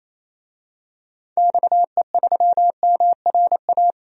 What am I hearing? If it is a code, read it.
Morse code, heard as a keyed tone, about 27 wpm, 700 Hz. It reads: XE3MRA
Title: DXCC-100 Call Signs - 6 Characters - 28wpm